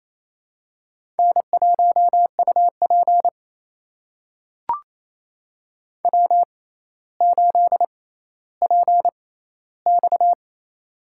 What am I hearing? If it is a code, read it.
Morse code, heard as a keyed tone, about 28 wpm, spaced as for 14 wpm, 700 Hz. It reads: N1UP W8PX
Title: USA Call Signs - 4 Characters - 6x Character Spacing - 28wpm